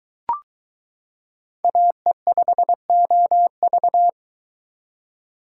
Morse code, heard as a keyed tone, about 22 wpm, 700 Hz. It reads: AE5OV